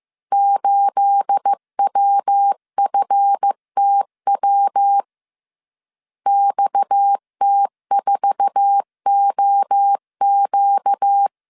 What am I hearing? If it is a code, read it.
Morse code, heard as a keyed tone, about 15 wpm, 800 Hz. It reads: 8WFTW XT4OQ
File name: CW / DET / 1 9 1500C S A 14wpm 3222